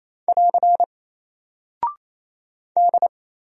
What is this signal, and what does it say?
Morse code, heard as a keyed tone, about 28 wpm, 700 Hz. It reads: <AR> D
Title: Single Character (Letter:Number:Punctuation:Prosign) Alt 28wpm